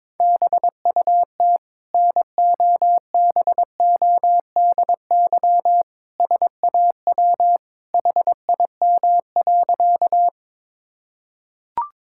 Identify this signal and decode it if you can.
Morse code, heard as a keyed tone, about 22 wpm, 700 Hz. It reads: BUT NOBODY SAW HIM.
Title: Sentences from Top 1600 Words 22wpm